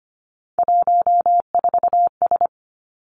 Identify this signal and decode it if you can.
Morse code, heard as a keyed tone, about 25 wpm, 700 Hz. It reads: 14S
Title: Three Challenging Letter-Number Combinations 25wpm